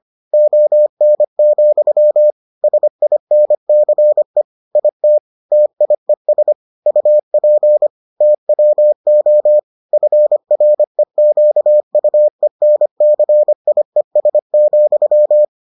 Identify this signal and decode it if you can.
Morse code, heard as a keyed tone, about 25 wpm, 600 Hz. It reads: ON, SINCE IT TIES UP TWO FREQUENCIES,